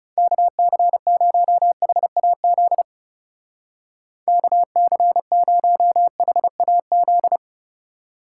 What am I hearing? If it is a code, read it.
Morse code, heard as a keyed tone, about 36 wpm, 700 Hz. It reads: KC0HAZ KC0HAZ
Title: Speed Racing - USA 2x3 Call Signs 30wpm